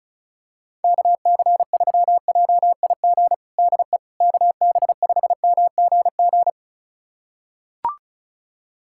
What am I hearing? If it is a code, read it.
Morse code, heard as a keyed tone, about 36 wpm, 700 Hz. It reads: KC3JIG DE KB5MGG
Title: USA Callsign DE Callsign 35wpm